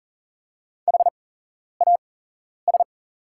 Morse code, heard as a keyed tone, about 39 wpm, spaced as for 11 wpm, 700 Hz. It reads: HAS
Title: 8x Character Spacing - Top 100 Words - 40wpm